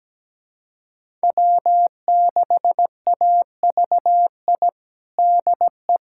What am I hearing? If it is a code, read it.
Morse code, heard as a keyed tone, about 17 wpm, 700 Hz. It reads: W6AVI DE
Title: USA Callsign DE Callsign - 5 Characters - 17wpm